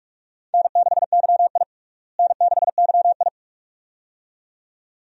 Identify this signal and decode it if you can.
Morse code, heard as a keyed tone, about 46 wpm, 700 Hz. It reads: N6YI N6YI
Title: USA Call Signs - 4 Characters - Repeated 2x - 45wpm